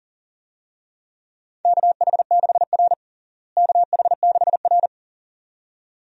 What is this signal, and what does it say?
Morse code, heard as a keyed tone, about 40 wpm, 700 Hz. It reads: KH6R KH6R
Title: DXCC-100 Call Signs - 4 Characters - Repeated 2x - 40wpm